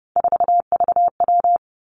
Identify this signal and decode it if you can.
Morse code, heard as a keyed tone, about 30 wpm, 700 Hz. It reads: 4VW
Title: Three Challenging Letter-Number Combinations 30wpm